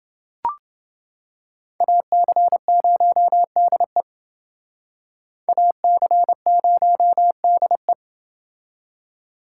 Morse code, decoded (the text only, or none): AC0DE AC0DE